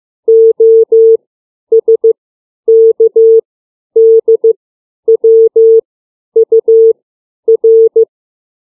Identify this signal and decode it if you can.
Morse code, heard as a keyed tone, about 15 wpm, 450 Hz. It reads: O S K D W U R
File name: CW / alphabet 4